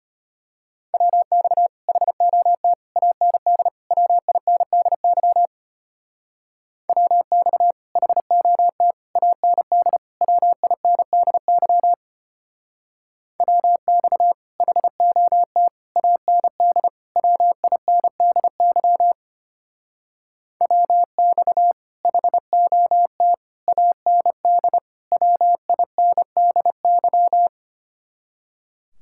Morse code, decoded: WX HOT AND WINDY WX HOT AND WINDY WX HOT AND WINDY WX HOT AND WINDY

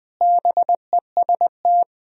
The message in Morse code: BEST

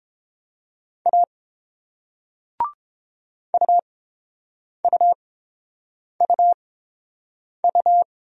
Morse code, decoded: A UUUU

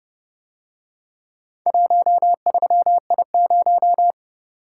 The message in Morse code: 13I0

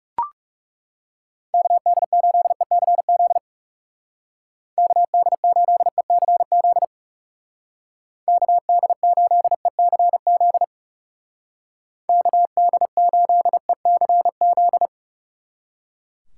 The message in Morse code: KD8ECZ KD8ECZ KD8ECZ KD8ECZ

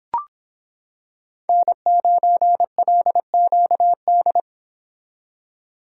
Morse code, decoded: N9LQD

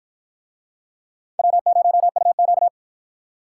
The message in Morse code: W0UQ

